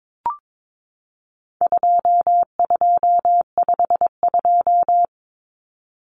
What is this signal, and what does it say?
Morse code, heard as a keyed tone, about 22 wpm, 700 Hz. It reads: 2252
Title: Common Cut Numbers - 4-Digit 22wpm